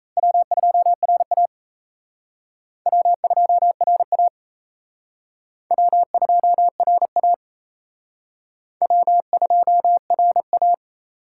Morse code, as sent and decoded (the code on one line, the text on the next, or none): .-- ..--- .-. .- / .-- ..--- .-. .- / .-- ..--- .-. .- / .-- ..--- .-. .-
W2RA W2RA W2RA W2RA